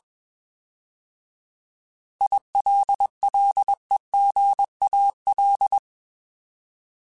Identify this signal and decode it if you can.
Morse code, heard as a keyed tone, about 21 wpm, 800 Hz. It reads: ILLEGAL